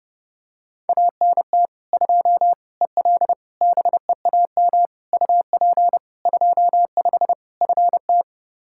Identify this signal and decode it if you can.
Morse code, heard as a keyed tone, about 30 wpm, 700 Hz. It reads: ANT 2 EL BEAM UP 25 FT
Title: Practice QSO - Exchange 2 - Info - 30wpm